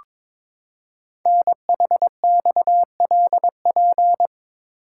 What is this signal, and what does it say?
Morse code, heard as a keyed tone, about 22 wpm, 700 Hz. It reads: NHXLP
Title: Code Groups - 5 Letter:Numbers - 22wpm